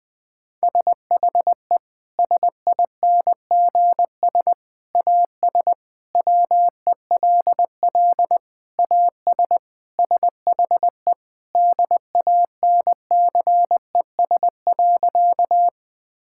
Morse code, heard as a keyed tone, about 20 wpm, 700 Hz. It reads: SHE SINGS AS WELL AS SHE DANCES.
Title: Sentences with 3-Word Phrases - Spoken Beginning - 20wpm